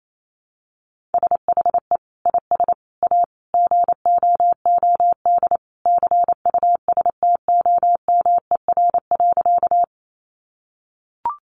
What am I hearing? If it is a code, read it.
Morse code, heard as a keyed tone, about 28 wpm, 700 Hz. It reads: SHE IS A GOOD CUSTOMER.
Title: Sentences from Top 1100 Words 28wpm